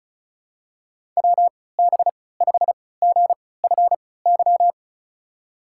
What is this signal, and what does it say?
Morse code, heard as a keyed tone, about 36 wpm, spaced as for 20 wpm, 700 Hz. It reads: WB5GFY
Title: USA Call Signs - 3x Character Spacing - 35wpm